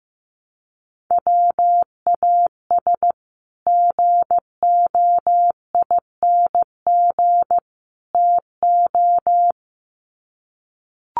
Morse code, heard as a keyed tone, about 15 wpm, 700 Hz. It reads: WAS GOING TO
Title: Sets of 3 Words 15wpm